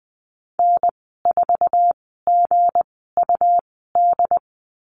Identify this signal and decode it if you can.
Morse code, heard as a keyed tone, about 20 wpm, spaced as for 15 wpm, 700 Hz. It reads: N4GUD
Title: 161 Lesson 26g - Letter g - Call Signs 20wpm